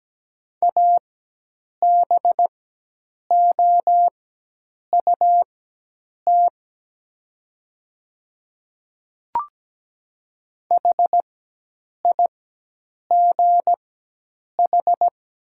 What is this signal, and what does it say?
Morse code, heard as a keyed tone, about 17 wpm, spaced as for 8 wpm, 700 Hz. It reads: ABOUT HIGH
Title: 4x Character Spacing - Top 200 Words - 17wpm